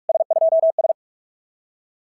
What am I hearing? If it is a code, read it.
Morse code, heard as a keyed tone, about 44 wpm, 650 Hz. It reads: S2S
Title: SOTA QSO Practice 45wpm